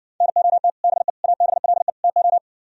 Morse code, heard as a keyed tone, about 59 wpm, 700 Hz. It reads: NOT BE ABLE TO